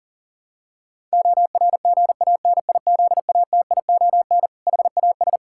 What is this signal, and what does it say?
Morse code, heard as a keyed tone, about 40 wpm, 700 Hz. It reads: ORGANIZATION HAS
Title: Sentences from Top 1000 Words - Review Part 5 - 40wpm